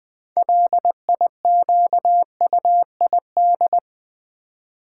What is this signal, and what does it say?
Morse code, heard as a keyed tone, about 20 wpm, 700 Hz. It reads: LIQUID
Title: 6 Letter Words - First 3 Letters Given - 20wpm